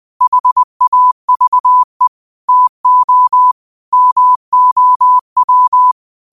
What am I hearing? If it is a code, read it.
Morse code, heard as a keyed tone, about 20 wpm, 1000 Hz. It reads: HAVE TO MOW
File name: 20wpm